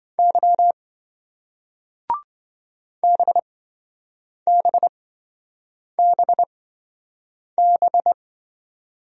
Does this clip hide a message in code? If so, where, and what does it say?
Yes: Morse code from 0.2 to 8.1 s, heard as a keyed tone, about 27 wpm, spaced as for 7 wpm, 700 Hz. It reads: Y BBBB